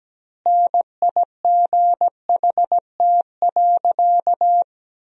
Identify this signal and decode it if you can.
Morse code, heard as a keyed tone, about 17 wpm, 700 Hz. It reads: NIGHT.
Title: 3x Word-Spacing - Sentences from Top 300 Words - 17wpm